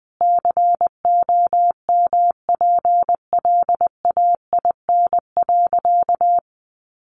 Morse code, heard as a keyed tone, about 20 wpm, 700 Hz. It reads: COMPLAIN.
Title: Sentences from Top 1300 Words 20wpm